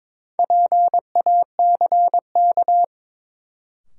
Morse code, heard as a keyed tone, about 22 wpm, 700 Hz. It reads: PACK
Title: Sentences from Top 1300 Words - New Word Drill - 22wpm